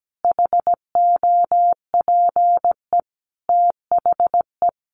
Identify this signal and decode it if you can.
Morse code, heard as a keyed tone, about 17 wpm, 700 Hz. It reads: HOPE THE
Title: Sentences from Top 1000 Words 17wpm